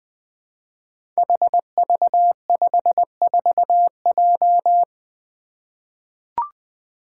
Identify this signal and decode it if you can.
Morse code, heard as a keyed tone, about 20 wpm, 700 Hz. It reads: HV54J
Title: Five Challenging Letter-Number Combinations 20wpm